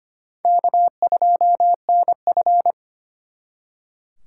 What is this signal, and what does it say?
Morse code, heard as a keyed tone, about 31 wpm, 700 Hz. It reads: K2NF
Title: Speed Racing - USA 1x2 Call Signs 25wpm